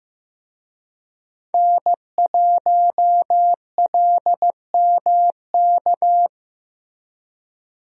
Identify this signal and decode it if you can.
Morse code, heard as a keyed tone, about 15 wpm, 700 Hz. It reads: N1LMK
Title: USA Call Signs - 5 Characters - Repeated 2x - 15wpm